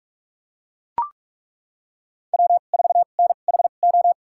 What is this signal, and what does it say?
Morse code, heard as a keyed tone, about 41 wpm, spaced as for 31 wpm, 700 Hz. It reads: W4NHO
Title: USA Call Signs - 5 Characters - 2x Character Spacing - 45wpm